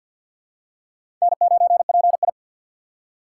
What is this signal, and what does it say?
Morse code, heard as a keyed tone, about 47 wpm, 700 Hz. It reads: N9PI
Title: USA Call Signs - 4x Character Spacing - 50wpm